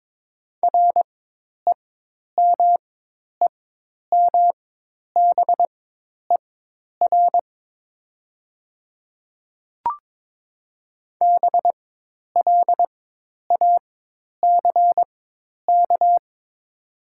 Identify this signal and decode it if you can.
Morse code, heard as a keyed tone, about 22 wpm, spaced as for 10 wpm, 700 Hz. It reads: REMEMBER BLACK
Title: 4x Character Spacing - Top 500 Words - 22wpm